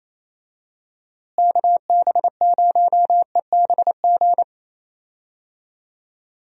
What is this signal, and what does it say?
Morse code, heard as a keyed tone, about 28 wpm, 700 Hz. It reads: KB0EBG
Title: USA Call Signs - 6 Characters - Repeated 2x - 28wpm